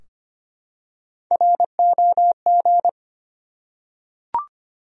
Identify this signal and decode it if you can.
Morse code, heard as a keyed tone, about 25 wpm, 700 Hz. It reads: ROG
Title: USA Common Names - Max 3 Letters - 25wpm